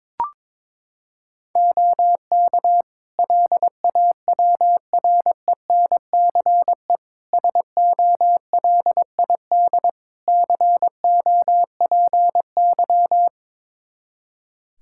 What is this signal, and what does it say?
Morse code, heard as a keyed tone, about 22 wpm, 700 Hz. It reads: OK LAWRENCE SOLID COPY
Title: Practice QSO - Exchange 3 - More Info - 22wpm